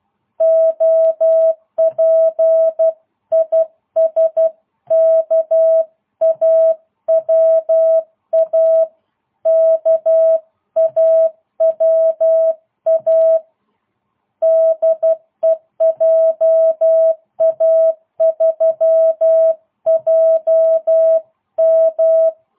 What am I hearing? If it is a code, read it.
Morse code, heard as a keyed tone, about 12 wpm, 660 Hz. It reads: OP IS KAWA KAWA DE JA3JM